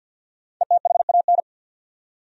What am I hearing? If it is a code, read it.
Morse code, heard as a keyed tone, about 48 wpm, 700 Hz. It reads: ETHAN
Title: USA Common Names - Max 5 Letters - 50wpm